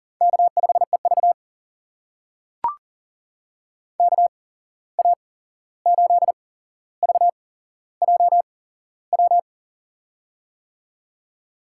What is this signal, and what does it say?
Morse code, heard as a keyed tone, about 38 wpm, spaced as for 15 wpm, 700 Hz. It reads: K5EV KA8VJW
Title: USA Call Signs - 8x Character Spacing - 40wpm